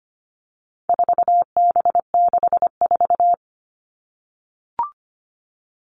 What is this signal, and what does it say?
Morse code, heard as a keyed tone, about 25 wpm, 700 Hz. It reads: 4B64
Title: All Cut Numbers - 4-Digit 25wpm